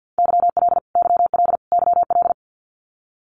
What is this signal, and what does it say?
Morse code, heard as a keyed tone, about 50 wpm, 700 Hz. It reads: YL YL YL
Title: Abbreviations Repeated 3x - 50wpm